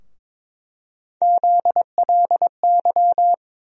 Morse code, heard as a keyed tone, about 22 wpm, 700 Hz. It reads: ZLY
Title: USA Call Sign Postfixes - 3 Letters 22wpm